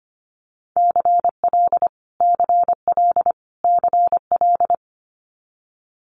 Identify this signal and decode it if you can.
Morse code, heard as a keyed tone, about 25 wpm, 700 Hz. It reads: CL CL CL
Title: Multi-letter Phonemes Repeated 3x 25wpm